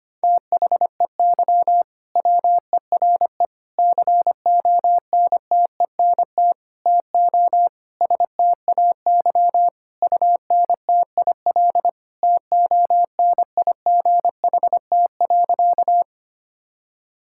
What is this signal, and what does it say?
Morse code, heard as a keyed tone, about 25 wpm, 700 Hz. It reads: THEY WERE CONTENT TO STAY UNTIL TONIGHT.
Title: Sentences from Top 1000 Words - Review Part 6 - 25wpm